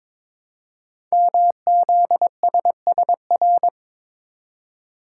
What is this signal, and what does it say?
Morse code, heard as a keyed tone, about 22 wpm, 700 Hz. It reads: MZSSR
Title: Code Groups - 5 Letters - 22wpm